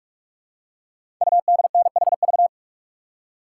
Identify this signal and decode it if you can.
Morse code, heard as a keyed tone, about 44 wpm, 700 Hz. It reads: UDNHV
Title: Code Groups - 5 Letters - 45wpm